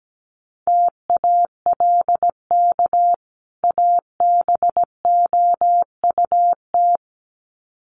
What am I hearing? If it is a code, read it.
Morse code, heard as a keyed tone, about 17 wpm, 700 Hz. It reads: TALK ABOUT
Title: Sets of 2 Words 17wpm